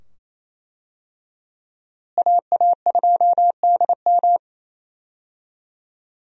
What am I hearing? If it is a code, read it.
Morse code, heard as a keyed tone, about 27 wpm, 700 Hz. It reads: AA2DM
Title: USA Call Signs - 5 Characters - Repeated 2x - 28wpm